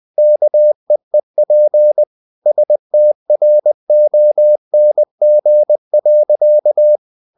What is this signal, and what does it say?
Morse code, heard as a keyed tone, about 20 wpm, 600 Hz. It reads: KEEP STRONG.
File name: the old man and the sea / the old man and the sea part 1 20wpm